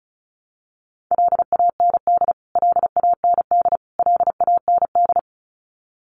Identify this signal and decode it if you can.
Morse code, heard as a keyed tone, about 34 wpm, 700 Hz. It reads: LAND LAND LAND
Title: Top 500 Words Repeated 35wpm